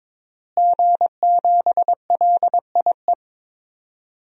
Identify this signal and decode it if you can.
Morse code, heard as a keyed tone, about 22 wpm, 700 Hz. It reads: G7LIE
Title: DXCC-100 Call Signs - 5 Characters - Repeated 2x - 22wpm